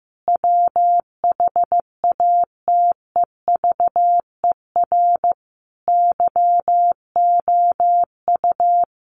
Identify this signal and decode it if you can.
Morse code, heard as a keyed tone, about 15 wpm, 700 Hz. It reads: WHATEVER YOU
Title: Sentences from Top 1500 Words 15wpm